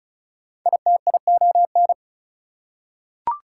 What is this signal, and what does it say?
Morse code, heard as a keyed tone, about 36 wpm, 700 Hz. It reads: ITION
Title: Prefixes and Suffixes 35wpm